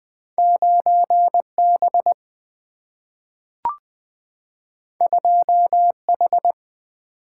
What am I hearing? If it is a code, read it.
Morse code, heard as a keyed tone, about 20 wpm, 700 Hz. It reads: 9B 2H
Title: Two Letter-Number Combinations - 20wpm